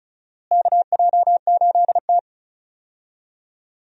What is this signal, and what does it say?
Morse code, heard as a keyed tone, about 36 wpm, 700 Hz. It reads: KJ8T
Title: USA Call Signs - 4 Characters - Repeated 2x - 35wpm